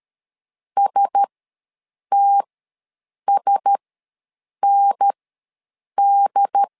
Morse code, heard as a keyed tone, about 13 wpm, spaced as for 7 wpm, 800 Hz. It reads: STSND
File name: LEZIONI / 4 Q1S ws 12wpm 1827